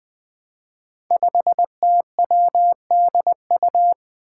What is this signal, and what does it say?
Morse code, heard as a keyed tone, about 20 wpm, 700 Hz. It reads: 5TWDU
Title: Five Challenging Letter-Number Combinations 20wpm